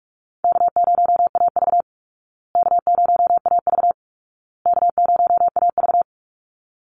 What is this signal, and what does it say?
Morse code, heard as a keyed tone, about 44 wpm, 700 Hz. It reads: K0AV K0AV K0AV
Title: Easy Call Signs Repeated 3x - 45wpm